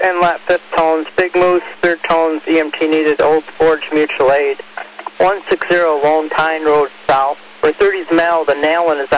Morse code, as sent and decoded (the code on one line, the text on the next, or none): none